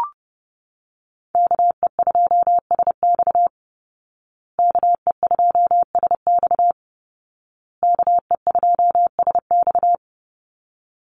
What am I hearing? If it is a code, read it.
Morse code, heard as a keyed tone, about 30 wpm, 700 Hz. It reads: KE2SX KE2SX KE2SX
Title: Call Signs Repeated 2x - 30wpm